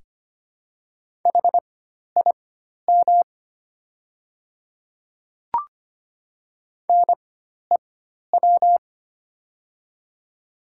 Morse code, heard as a keyed tone, about 25 wpm, spaced as for 12 wpm, 690 Hz. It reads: HIM NEW
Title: 4x Character Spacing - Top 200 Words - 25wpm